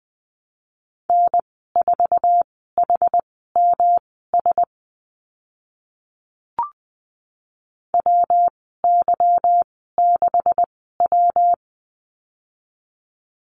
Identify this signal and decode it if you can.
Morse code, heard as a keyed tone, about 20 wpm, spaced as for 14 wpm, 700 Hz. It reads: N4HMS WY6W
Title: 161 Lesson 26g - Letter g - Call Signs 20wpm